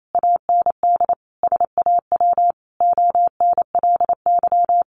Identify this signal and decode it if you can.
Morse code, heard as a keyed tone, about 28 wpm, 700 Hz. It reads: AND SAW ONLY
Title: Sentences from Top 1200 Words 28wpm